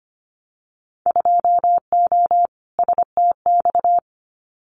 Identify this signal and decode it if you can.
Morse code, heard as a keyed tone, about 25 wpm, 700 Hz. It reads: 2O STX
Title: Winter Field Day Exchanges 25wpm